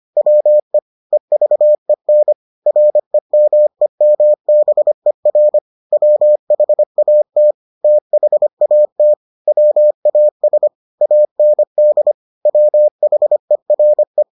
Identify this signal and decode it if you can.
Morse code, heard as a keyed tone, about 25 wpm, 600 Hz. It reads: WE EVEN REMEMBER WHAT THAT WAS AND WHERE